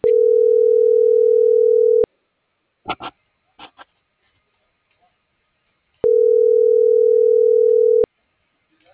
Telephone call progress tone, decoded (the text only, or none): ringback tone